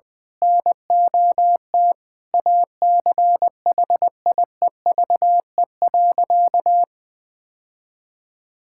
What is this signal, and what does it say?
Morse code, heard as a keyed tone, about 21 wpm, 700 Hz. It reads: NOT ACHIEVE.
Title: Sentences from Top 1000 Words - Review Part 6 - 20wpm